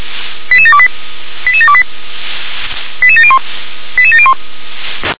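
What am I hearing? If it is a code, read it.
Five-tone selective call (ZVEI), heard as quick runs of five tones, tones of about 70 ms. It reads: 88728 88728 88701 88701
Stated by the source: Five-tone ZVEI selective call received on an SDR.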